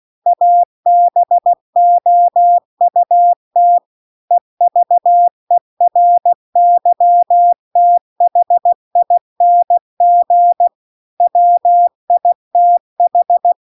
Morse code, heard as a keyed tone, about 16 wpm, 700 Hz. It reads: ABOUT EVERYTHING WITH